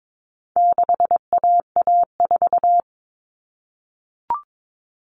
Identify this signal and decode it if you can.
Morse code, heard as a keyed tone, about 22 wpm, 700 Hz. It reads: 6AA4
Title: All Cut Numbers - 4-Digit 22wpm